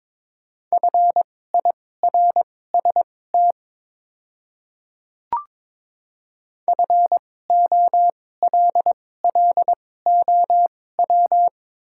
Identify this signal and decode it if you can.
Morse code, heard as a keyed tone, about 22 wpm, spaced as for 16 wpm, 700 Hz. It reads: FIRST FOLLOW
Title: 2x Character Spacing - Top 300 Words - 22wpm